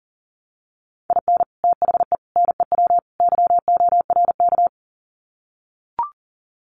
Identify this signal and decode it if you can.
Morse code, heard as a keyed tone, about 40 wpm, 700 Hz. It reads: IN THE NEW YORK
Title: Sets of 4 Words 40wpm